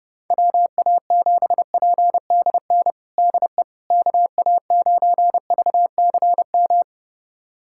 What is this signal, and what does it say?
Morse code, heard as a keyed tone, about 30 wpm, 700 Hz. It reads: WA7PDN DE KA9VCM
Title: USA Callsign DE Callsign 30wpm